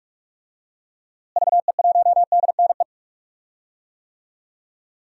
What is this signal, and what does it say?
Morse code, heard as a keyed tone, about 41 wpm, 700 Hz. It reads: VE1DNE
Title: DXCC-100 Call Signs - 6 Characters - Repeated 2x - 45wpm